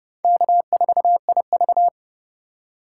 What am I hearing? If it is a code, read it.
Morse code, heard as a keyed tone, about 29 wpm, 700 Hz. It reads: K4IV